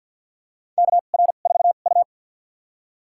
Morse code, heard as a keyed tone, about 53 wpm, spaced as for 37 wpm, 700 Hz. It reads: KR4U
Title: USA Call Signs - 4 Characters - 2x Character Spacing - 50wpm